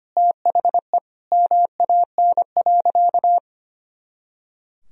Morse code, heard as a keyed tone, about 25 wpm, 700 Hz. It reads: THE MAN.
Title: Sentences from Top 200 Words and Abbreviations 25wpm